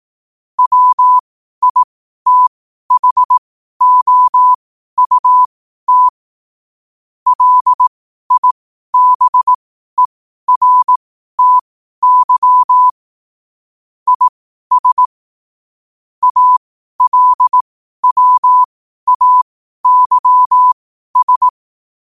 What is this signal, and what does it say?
Morse code, heard as a keyed tone, about 18 wpm, spaced as for 12 wpm, 1000 Hz. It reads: WITHOUT LIBERTY IS ALWAYS
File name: qotdfiles / 12wpm